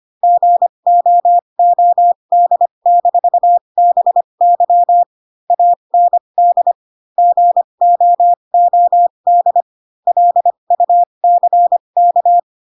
Morse code, heard as a keyed tone, about 25 wpm, spaced as for 22 wpm, 700 Hz. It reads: GOOD-BY AND GOOD LUCK